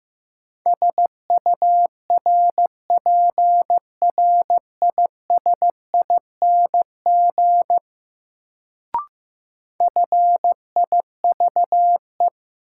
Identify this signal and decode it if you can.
Morse code, heard as a keyed tone, about 15 wpm, 700 Hz. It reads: SURPRISING FIVE
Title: Sets of 5 Words - New Word Drill - 15wpm